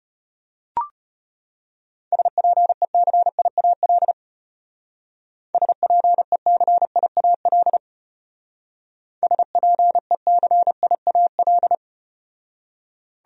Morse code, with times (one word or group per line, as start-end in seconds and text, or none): 2.1-4.1 SPECIAL
5.5-7.8 SPECIAL
9.2-11.8 SPECIAL